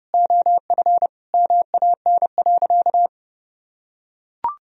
Morse code, heard as a keyed tone, about 30 wpm, 700 Hz. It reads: OF MAN.
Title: Sentences from Top 1000 Words - Review Part 2 - 30wpm